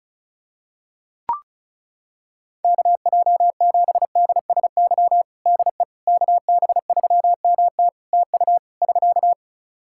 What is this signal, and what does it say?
Morse code, heard as a keyed tone, about 36 wpm, 700 Hz. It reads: KJ7DSY DE KB3MT TU <SK>
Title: K1USN SST QSO Practice 35wpm